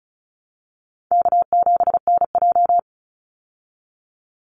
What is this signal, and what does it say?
Morse code, heard as a keyed tone, about 34 wpm, 700 Hz. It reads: K7NJ
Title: Call Signs Repeated 2x - 35wpm